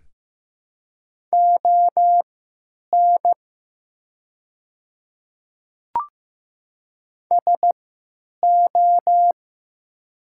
Morse code, heard as a keyed tone, about 16 wpm, spaced as for 9 wpm, 700 Hz. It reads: ON SO
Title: 3x Character Spacing - Top 100 Words - 15wpm